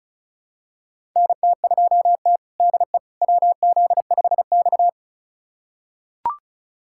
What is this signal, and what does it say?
Morse code, heard as a keyed tone, about 36 wpm, 700 Hz. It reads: NT2T DE WZ5X
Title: USA Callsign DE Callsign - 4 Characters - 35wpm